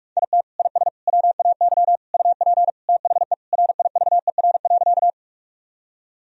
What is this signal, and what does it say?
Morse code, heard as a keyed tone, about 41 wpm, 700 Hz. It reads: IT IS WAY UP THE RIVER.
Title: Sentences from Top 1000 Words - Review Part 1 - 45wpm